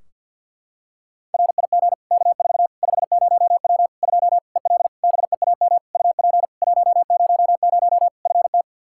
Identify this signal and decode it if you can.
Morse code, heard as a keyed tone, about 50 wpm, 700 Hz. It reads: RIG K4 50W 2 EL BEAM UP 100 FT
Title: Practice QSO - Exchange 2 - Info - 50wpm